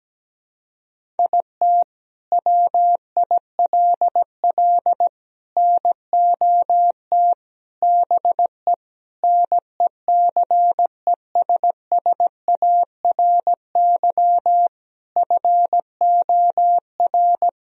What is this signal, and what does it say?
Morse code, heard as a keyed tone, about 17 wpm, 700 Hz. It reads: IT WILL NOT BE NECESSARY FOR
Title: Sentences from Top 1000 Words - Review Part 1 - 17wpm